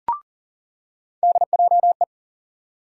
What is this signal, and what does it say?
Morse code, heard as a keyed tone, about 40 wpm, 700 Hz. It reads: DJE
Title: USA Call Sign Postfixes - 3 Letters 40wpm